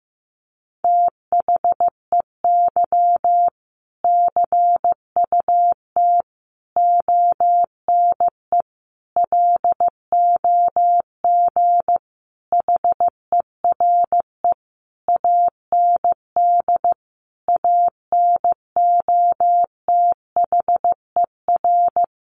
Morse code, heard as a keyed tone, about 15 wpm, 700 Hz. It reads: THEY CUT ONE LOG HERE AND ANOTHER
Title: Sentences from Top 1400 Words 15wpm